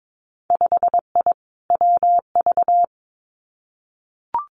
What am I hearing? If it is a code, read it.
Morse code, heard as a keyed tone, about 22 wpm, 700 Hz. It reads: 5I WV
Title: Winter Field Day Exchanges Encore 22wpm